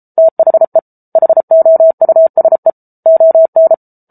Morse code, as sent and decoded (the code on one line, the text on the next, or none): - .... / .... --- ..- ... / --- -.
THE HOUSE ON